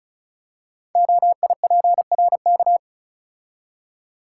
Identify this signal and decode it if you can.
Morse code, heard as a keyed tone, about 36 wpm, 690 Hz. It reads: OIPRK
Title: Code Groups - 5 Letters - 35wpm